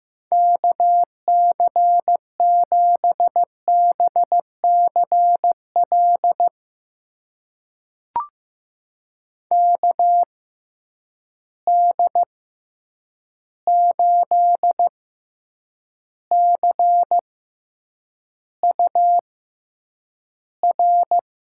Morse code, heard as a keyed tone, about 15 wpm, spaced as for 8 wpm, 700 Hz. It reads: KC7BCL KD8CUR
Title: USA Call Signs - 6 Characters - 6x Character Spacing - 15wpm